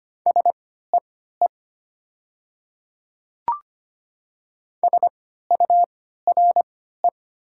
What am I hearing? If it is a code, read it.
Morse code, heard as a keyed tone, about 25 wpm, spaced as for 14 wpm, 700 Hz. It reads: SEE SURE
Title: 3x Character Spacing - Top 300 Words - 25wpm